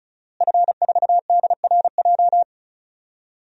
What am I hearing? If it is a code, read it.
Morse code, heard as a keyed tone, about 34 wpm, 700 Hz. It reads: F4DRJ
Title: DXCC-100 Call Signs - 5 Characters - Repeated 3x - 35wpm